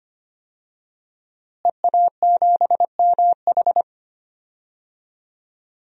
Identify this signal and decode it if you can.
Morse code, heard as a keyed tone, about 25 wpm, 700 Hz. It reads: EA7MH
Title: DXCC-100 Call Signs - 5 Characters - Repeated 2x - 25wpm